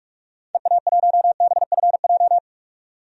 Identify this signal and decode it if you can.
Morse code, heard as a keyed tone, about 46 wpm, 700 Hz. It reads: EA1BFJ
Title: DXCC-100 Call Signs - 6 Characters - 45wpm